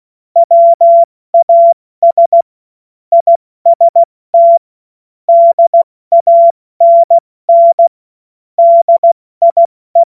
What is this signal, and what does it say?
Morse code, heard as a keyed tone, about 16 wpm, 680 Hz. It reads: WAS IST DANN DIE